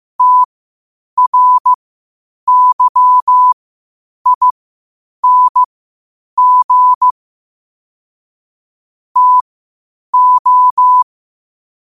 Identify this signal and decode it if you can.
Morse code, heard as a keyed tone, about 15 wpm, spaced as for 8 wpm, 1000 Hz. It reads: TRYING TO
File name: qotdfiles / 8wpm